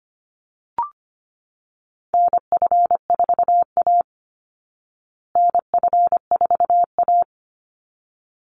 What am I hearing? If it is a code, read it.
Morse code, heard as a keyed tone, about 25 wpm, 700 Hz. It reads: NF4A NF4A
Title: Call Signs Repeated 3x - 25wpm